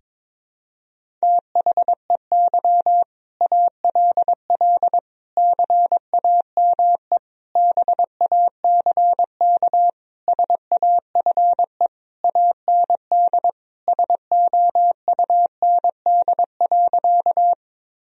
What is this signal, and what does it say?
Morse code, heard as a keyed tone, about 22 wpm, 700 Hz. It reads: THEY ALL CAME BACK SAFE AND SOUND.
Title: Sentences from Top 1000 Words - Review Part 1 - 22wpm